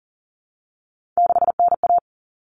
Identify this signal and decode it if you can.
Morse code, heard as a keyed tone, about 40 wpm, 700 Hz. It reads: 6NA
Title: Common Cut Numbers - 3-Digit 40wpm